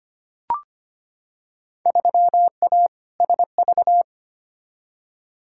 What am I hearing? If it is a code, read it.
Morse code, heard as a keyed tone, about 25 wpm, 700 Hz. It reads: 3A SV
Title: Field Day Exchanges - 25wpm